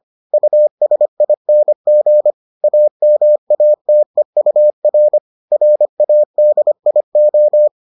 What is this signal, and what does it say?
Morse code, heard as a keyed tone, about 25 wpm, 600 Hz. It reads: USING AMATEUR RADIO